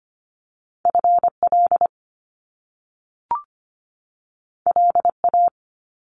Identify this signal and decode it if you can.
Morse code, heard as a keyed tone, about 25 wpm, 700 Hz. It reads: FL LA
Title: US States 25wpm